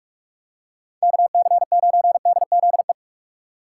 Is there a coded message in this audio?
Yes: Morse code from 1.0 to 2.9 s, heard as a keyed tone, about 44 wpm, 700 Hz. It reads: KC9DZE